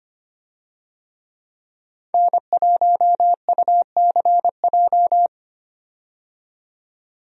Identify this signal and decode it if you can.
Morse code, heard as a keyed tone, about 25 wpm, 700 Hz. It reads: N1UCJ
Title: USA Call Signs - 5 Characters - 25wpm